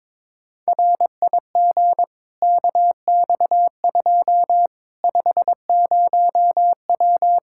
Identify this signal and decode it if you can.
Morse code, heard as a keyed tone, about 22 wpm, 700 Hz. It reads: RIG KX2 50W
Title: K1USN SST QSO Practice 22wpm